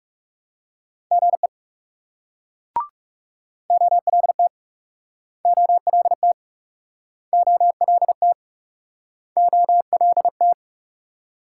Morse code, heard as a keyed tone, about 40 wpm, 700 Hz. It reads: GE OLT OLT OLT OLT